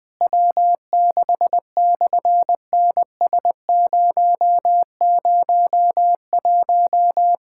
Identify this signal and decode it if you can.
Morse code, heard as a keyed tone, about 19 wpm, 700 Hz. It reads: W6/NS001
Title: SOTA QSO Practice 20wpm